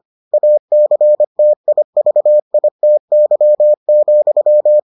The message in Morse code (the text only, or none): ACTIVITY,